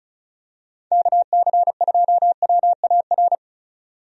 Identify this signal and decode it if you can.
Morse code, heard as a keyed tone, about 36 wpm, 700 Hz. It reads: KC2WAR
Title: USA Call Signs - 6 Characters - 35wpm